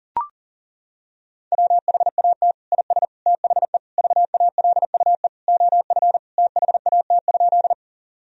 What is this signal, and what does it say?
Morse code, heard as a keyed tone, about 40 wpm, 700 Hz. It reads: WHAT IS THE VALUE OF THAT?
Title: Sentences from Top 1000 Words - Review Part 3 - 40wpm